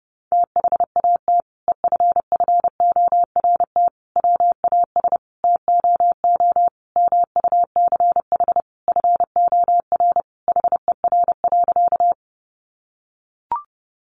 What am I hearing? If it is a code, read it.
Morse code, heard as a keyed tone, about 30 wpm, 700 Hz. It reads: THAT EFFORT WAS TOO MUCH FOR HER.
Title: Sentences from Top 1200 Words 30wpm